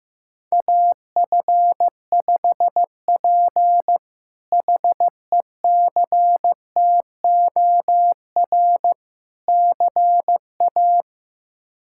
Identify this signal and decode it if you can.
Morse code, heard as a keyed tone, about 15 wpm, 700 Hz. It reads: AF5P HECTOR CA
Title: K1USN SST QSO Practice - 15wpm